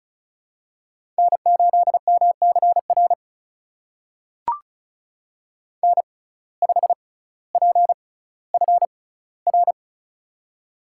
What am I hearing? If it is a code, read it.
Morse code, heard as a keyed tone, about 36 wpm, spaced as for 18 wpm, 700 Hz. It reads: N8MCR N5PFR
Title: USA Call Signs - 5 Characters - 6x Character Spacing - 35wpm